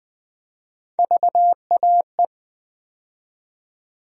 Morse code, heard as a keyed tone, about 20 wpm, 700 Hz. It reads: VAE
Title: USA Call Sign Postfixes - 3 Letters 20wpm